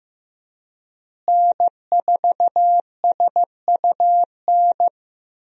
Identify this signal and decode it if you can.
Morse code, heard as a keyed tone, about 16 wpm, 700 Hz. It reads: N4SUN
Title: USA Call Signs - 5 Characters - Repeated 3x - 15wpm